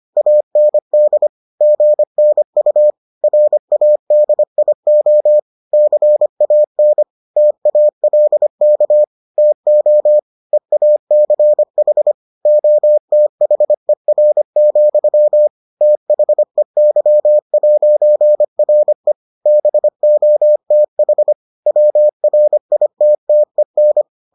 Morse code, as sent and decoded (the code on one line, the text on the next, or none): .- -. -.. / --. -. ..- / .-. .- -.. .. --- / -.-. .- -. / - .- .-.. -.- / - --- / . .- -.-. .... / --- - .... .-. --..-- / - .... -.-- .----. .-. / -... --- - .... / .-- .-. .. - - . -.
AND GNU RADIO CAN TALK TO EACH OTHER, THEY'RE BOTH WRITTEN